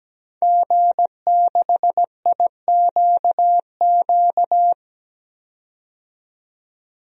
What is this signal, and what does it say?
Morse code, heard as a keyed tone, about 17 wpm, 700 Hz. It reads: G6IQQ
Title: DXCC-100 Call Signs - 5 Characters - 17wpm